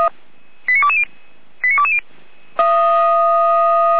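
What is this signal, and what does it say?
Five-tone selective call (ZVEI), heard as quick runs of five tones, tones of about 70 ms. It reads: 89220 89220
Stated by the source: ZVEI five-tone call 89223 received by SDR, Feuerwehr Orlamünde.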